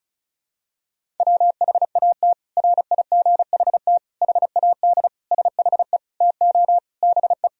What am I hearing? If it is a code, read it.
Morse code, heard as a keyed tone, about 36 wpm, 700 Hz. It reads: WHAT RIGHT HAD SHE TO BE
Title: Sentences from Top 1000 Words - Review Part 1 - 35wpm